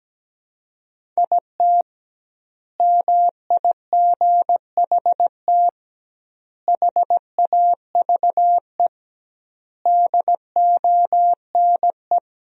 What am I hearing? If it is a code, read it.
Morse code, heard as a keyed tone, about 17 wpm, 700 Hz. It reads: IT MIGHT HAVE DONE
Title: 2x Word-Spacing - Sentences from Top 500 Words - 17wpm